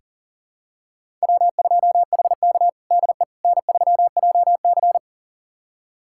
Morse code, heard as a keyed tone, about 40 wpm, 700 Hz. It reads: W2HK DE N3JC
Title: USA Callsign DE Callsign - 4 Characters - 40wpm